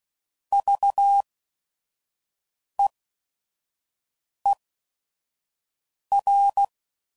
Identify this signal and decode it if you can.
Morse code, heard as a keyed tone, about 16 wpm, spaced as for 5 wpm, 800 Hz. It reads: VEER
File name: Prefixes / By First Letter / Vs